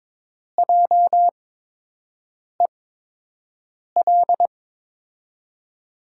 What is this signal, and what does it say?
Morse code, heard as a keyed tone, about 22 wpm, spaced as for 6 wpm, 700 Hz. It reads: JEL
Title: USA Call Signs - 6 Characters - 8x Character Spacing - 22wpm